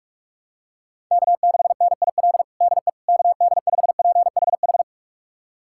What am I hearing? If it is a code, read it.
Morse code, heard as a keyed tone, about 42 wpm, 700 Hz. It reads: K6NIL DE KD5PHH
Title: USA Callsign DE Callsign 45wpm